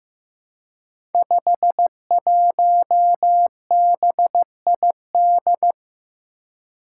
Morse code, heard as a keyed tone, about 15 wpm, 700 Hz. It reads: 51BID
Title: Five Challenging Letter-Number Combinations 15wpm